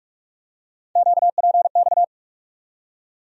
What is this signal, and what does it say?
Morse code, heard as a keyed tone, about 46 wpm, 700 Hz. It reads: QPX
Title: USA Call Sign Postfixes - 3 Letters 45wpm